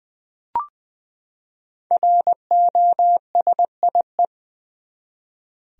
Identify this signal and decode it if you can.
Morse code, heard as a keyed tone, about 20 wpm, 700 Hz. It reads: ROSIE
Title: USA Common Names - 20wpm